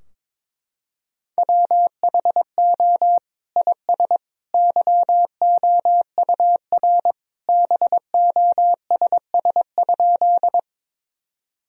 Morse code, heard as a keyed tone, about 22 wpm, 700 Hz. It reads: WHO IS YOUR BOSS?
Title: Sentences from Top 1000 Words - Review Part 4 - 22wpm